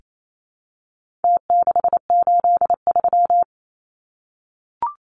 Morse code, heard as a keyed tone, about 28 wpm, 700 Hz. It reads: T683
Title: Numbers - Contest Style - 4-Digit 28wpm